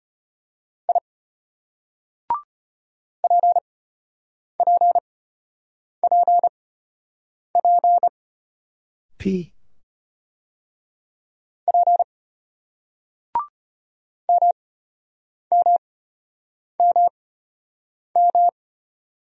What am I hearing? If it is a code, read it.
Morse code, heard as a keyed tone, about 31 wpm, spaced as for 8 wpm, 700 Hz. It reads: I PPPP P MMMM
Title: Speed-Racing Single Letters 25wpm